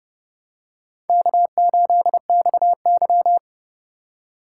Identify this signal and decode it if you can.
Morse code, heard as a keyed tone, about 31 wpm, 700 Hz. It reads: K8XY